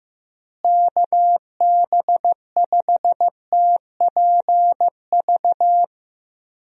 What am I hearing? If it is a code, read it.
Morse code, heard as a keyed tone, about 15 wpm, 700 Hz. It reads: KB5TPV
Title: USA Call Signs - 6 Characters - Repeated 3x - 15wpm